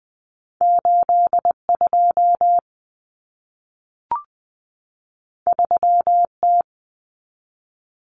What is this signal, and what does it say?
Morse code, heard as a keyed tone, about 20 wpm, 700 Hz. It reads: 82 3T
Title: Common Cut Numbers - 2-Digit 20wpm